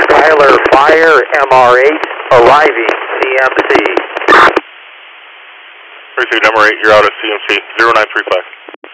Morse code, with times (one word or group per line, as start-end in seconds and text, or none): none